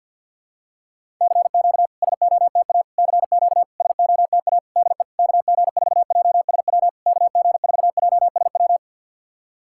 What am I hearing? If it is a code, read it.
Morse code, heard as a keyed tone, about 50 wpm, 700 Hz. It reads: CQ SOTA CQ SOTA DE KG4JSW KG4JSW
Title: SOTA QSO Practice 50wpm